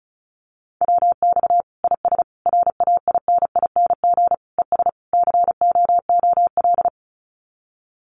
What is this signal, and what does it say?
Morse code, heard as a keyed tone, about 34 wpm, 700 Hz. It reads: WX IS RAINING ES COOL
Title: Common QSO Phrases 35wpm